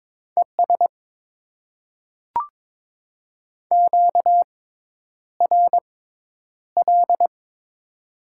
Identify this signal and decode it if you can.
Morse code, heard as a keyed tone, about 23 wpm, spaced as for 10 wpm, 700 Hz. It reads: ES QRL
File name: Top 20 QSO Elements - 6x Character Spacing - 22wpm